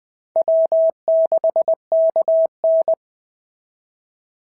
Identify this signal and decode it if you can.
Morse code, heard as a keyed tone, about 21 wpm, 690 Hz. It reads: W6KN
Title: SOTA QSO Practice 20wpm